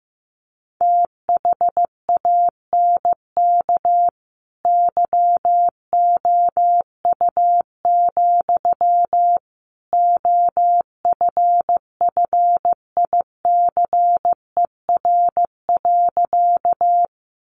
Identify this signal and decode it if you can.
Morse code, heard as a keyed tone, about 15 wpm, 700 Hz. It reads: THANK YOU, OFFICER.
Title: Sentences from Top 1200 Words 15wpm